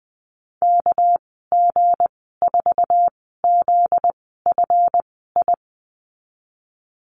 Morse code, heard as a keyed tone, about 20 wpm, spaced as for 15 wpm, 700 Hz. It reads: KG4ZFI
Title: 245 Lesson 39g - Letter z - Call Signs 20wpm